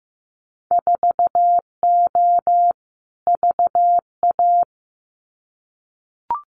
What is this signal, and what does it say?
Morse code, heard as a keyed tone, about 15 wpm, 700 Hz. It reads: 4O VA
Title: Winter Field Day Exchanges Encore 15wpm